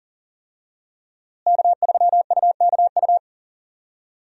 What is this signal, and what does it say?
Morse code, heard as a keyed tone, about 40 wpm, 700 Hz. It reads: K3UKU
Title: USA Call Signs - 5 Characters - Repeated 3x - 40wpm